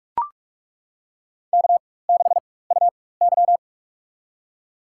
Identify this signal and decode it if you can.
Morse code, heard as a keyed tone, about 42 wpm, spaced as for 20 wpm, 700 Hz. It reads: K6UY
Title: USA Call Signs - 4x Character Spacing - 45wpm